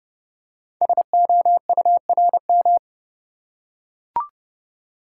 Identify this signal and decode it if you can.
Morse code, heard as a keyed tone, about 28 wpm, 700 Hz. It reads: SOURM
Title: Code Groups - 5 Letters - 30wpm